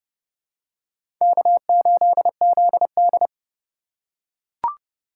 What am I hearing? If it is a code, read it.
Morse code, heard as a keyed tone, about 30 wpm, 700 Hz. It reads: K8ZD
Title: USA Call Signs - 4 Characters - 8x Character Spacing - 30wpm